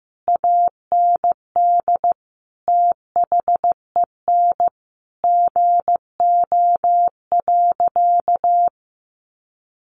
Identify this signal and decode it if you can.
Morse code, heard as a keyed tone, about 15 wpm, 700 Hz. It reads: AND THEN GO.
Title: Sentences from Top 1300 Words 15wpm